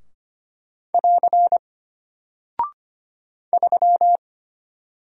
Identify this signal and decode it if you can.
Morse code, heard as a keyed tone, about 25 wpm, 700 Hz. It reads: <AR> 3
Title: Single Character (Letter:Number:Punctuation:Prosign) Alt 25wpm